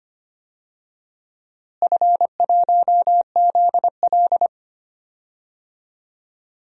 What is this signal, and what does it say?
Morse code, heard as a keyed tone, about 25 wpm, 700 Hz. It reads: F1ZL
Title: DXCC-100 Call Signs - 4 Characters - Repeated 2x - 25wpm